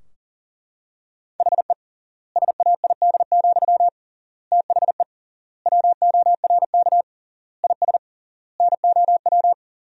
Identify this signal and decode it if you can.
Morse code, heard as a keyed tone, about 40 wpm, 700 Hz. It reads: HE SAID, THE WORK IS NOW
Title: 3x Word-Spacing - Sentences from Top 400 Words - 40wpm